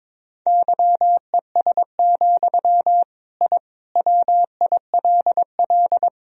Morse code, heard as a keyed tone, about 22 wpm, 700 Hz. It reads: YES, I WILL
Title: Sentences from Top 1000 Words - Review Part 2 - 22wpm